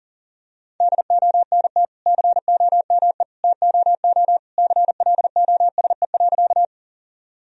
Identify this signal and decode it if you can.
Morse code, heard as a keyed tone, about 40 wpm, 700 Hz. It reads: DONT COME TOO CLOSE.
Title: Sentences from Top 1000 Words - Review Part 1 - 40wpm